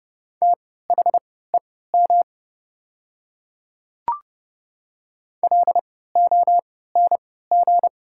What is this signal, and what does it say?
Morse code, heard as a keyed tone, about 30 wpm, spaced as for 17 wpm, 700 Hz. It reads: THEM LONG
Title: 3x Character Spacing - Top 100 Words - 30wpm